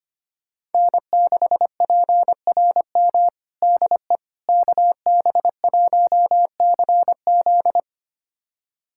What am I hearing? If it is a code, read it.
Morse code, heard as a keyed tone, about 25 wpm, 700 Hz. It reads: N6PRM DE KB1CZ
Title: USA Callsign DE Callsign - 5 Characters - 25wpm